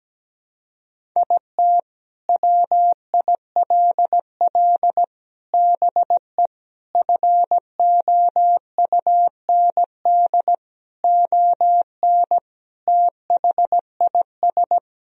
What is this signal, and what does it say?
Morse code, heard as a keyed tone, about 17 wpm, 700 Hz. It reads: IT WILL BE FOUND ON THIS
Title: Sentences from Top 1000 Words - Review Part 1 - 17wpm